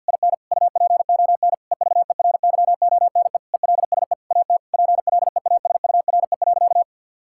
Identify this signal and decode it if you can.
Morse code, heard as a keyed tone, about 50 wpm, 700 Hz. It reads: IN UPON EVERYONE ELSE AT PLEASURE.
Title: Sentences from Top 1600 Words 50wpm